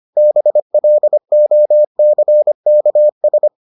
Morse code, heard as a keyed tone, about 25 wpm, 600 Hz. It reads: BLOCKS